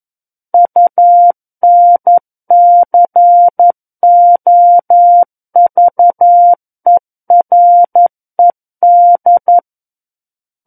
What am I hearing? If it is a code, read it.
Morse code, heard as a keyed tone, about 11 wpm, 700 Hz. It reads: UNCOVERED
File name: 11 WPM